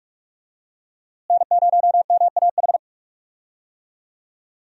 Morse code, heard as a keyed tone, about 46 wpm, 700 Hz. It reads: N0MAH
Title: USA Call Signs - 5 Characters - Repeated 3x - 45wpm